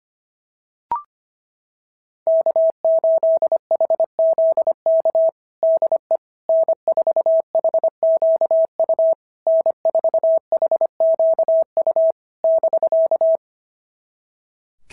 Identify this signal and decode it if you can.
Morse code, heard as a keyed tone, about 25 wpm, 660 Hz. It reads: K8HZK DE N4HQU N4HQU <BK>
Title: SOTA QSO Practice 25wpm